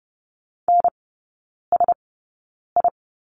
Morse code, heard as a keyed tone, about 30 wpm, spaced as for 9 wpm, 700 Hz. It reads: NSI
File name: course / 30:10wpm / 087 Lesson 15e - Number 5 - Three Characters 30wpm